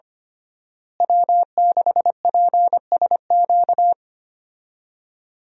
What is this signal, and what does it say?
Morse code, heard as a keyed tone, about 25 wpm, 700 Hz. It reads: W6PSQ
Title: USA Call Signs - 5 Characters - Repeated 3x - 25wpm